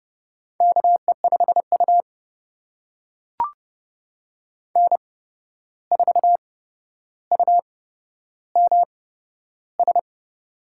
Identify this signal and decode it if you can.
Morse code, heard as a keyed tone, about 30 wpm, spaced as for 12 wpm, 700 Hz. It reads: KE5U N4UMS
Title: USA Call Signs - 8x Character Spacing - 30wpm